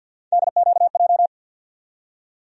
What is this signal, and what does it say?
Morse code, heard as a keyed tone, about 53 wpm, 700 Hz. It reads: DQJ
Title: USA Call Sign Postfixes - 3 Letters 50wpm